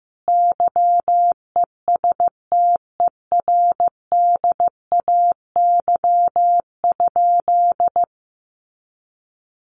Morse code, heard as a keyed tone, about 15 wpm, 700 Hz. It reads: YESTERDAY?
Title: A Sound Of Thunder 15wpm